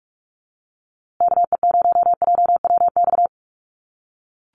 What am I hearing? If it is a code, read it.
Morse code, heard as a keyed tone, about 44 wpm, 700 Hz. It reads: KE0JWX
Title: USA Call Signs Repeated 2x 45wpm